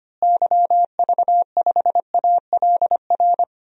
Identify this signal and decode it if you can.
Morse code, heard as a keyed tone, about 25 wpm, 700 Hz. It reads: YV5ALR
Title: DXCC-100 Call Signs - 6 Characters - Repeated 2x - 25wpm